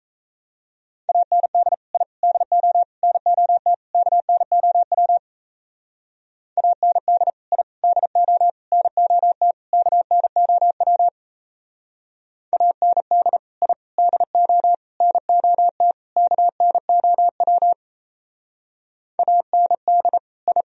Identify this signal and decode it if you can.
Morse code, heard as a keyed tone, about 36 wpm, 700 Hz. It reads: AND I DO NOT KNOW AND I DO NOT KNOW AND I DO NOT KNOW AND I